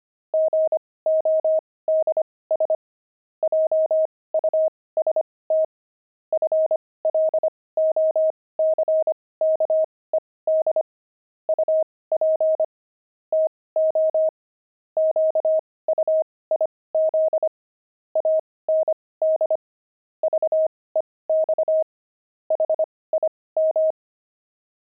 Morse code, heard as a keyed tone, about 25 wpm, spaced as for 18 wpm, 630 Hz. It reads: GODS JUST FLOCKED UP TO QUIZ AND VEX HIM
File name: pangram-fs-18-cs-25-brawny